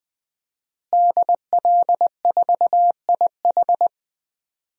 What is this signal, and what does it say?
Morse code, heard as a keyed tone, about 20 wpm, 700 Hz. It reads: DL4IH
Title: DXCC-100 Call Signs - 5 Characters - Repeated 3x - 20wpm